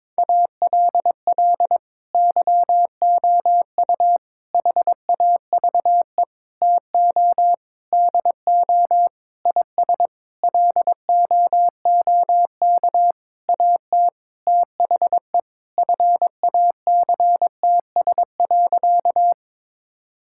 Morse code, heard as a keyed tone, about 22 wpm, 700 Hz. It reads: ALL YOU HAVE TO DO IS LOOK AT THE FACTS.
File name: practice / 22wpm / Sentences with 8-Word Phrases - 22wpm